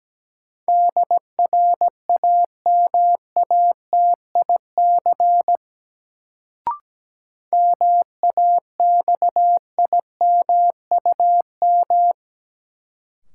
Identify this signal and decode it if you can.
Morse code, heard as a keyed tone, about 17 wpm, 700 Hz. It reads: DRAMATIC MAXIMUM
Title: Sentences from Top 1300 Words - New Word Drill Alt - 17wpm